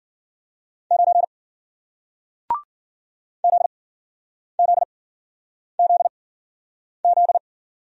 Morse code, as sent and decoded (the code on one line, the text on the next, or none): ----. / --.. --.. --.. --..
9 ZZZZ